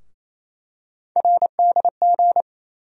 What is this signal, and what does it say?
Morse code, heard as a keyed tone, about 28 wpm, 700 Hz. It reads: RDG
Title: USA Call Sign Postfixes - 3 Letters 28wpm